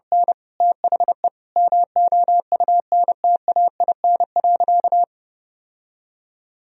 Morse code, heard as a keyed tone, about 32 wpm, 700 Hz. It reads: N THE MOUNTAIN.
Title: Sentences from Top 1000 Words - Review Part 3 - 30wpm